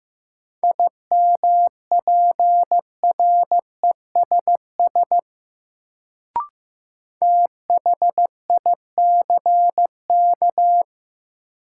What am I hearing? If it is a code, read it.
Morse code, heard as a keyed tone, about 16 wpm, 700 Hz. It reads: IMPRESS THICK
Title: Sentences from Top 1600 Words - New Word Drill - 15wpm